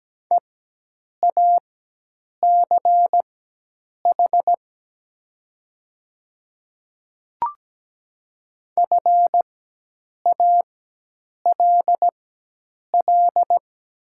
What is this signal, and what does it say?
Morse code, heard as a keyed tone, about 17 wpm, spaced as for 8 wpm, 700 Hz. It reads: EACH FALL
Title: 4x Character Spacing - Top 500 Words - 17wpm